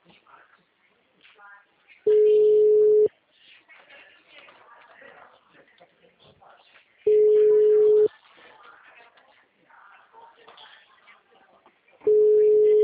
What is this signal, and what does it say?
Ringback tone (Europe, 425 Hz), with speech.